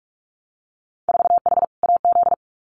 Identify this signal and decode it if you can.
Morse code, heard as a keyed tone, about 44 wpm, 690 Hz. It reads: 4H AZ
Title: Winter Field Day Exchanges Encore 45wpm